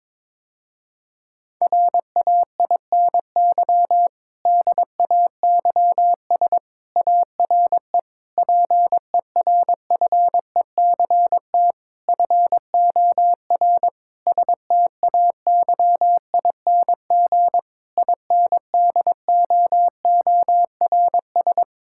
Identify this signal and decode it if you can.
Morse code, heard as a keyed tone, about 22 wpm, 700 Hz. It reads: RAINY DAYS ARE PERFECT FOR STAYING INDOORS
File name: Difficult Sentences with Hints - 22wpm